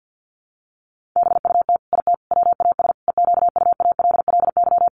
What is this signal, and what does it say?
Morse code, heard as a keyed tone, about 50 wpm, 700 Hz. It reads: BUT IT WAS EQUALLY
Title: Sentences from Top 1400 Words 50wpm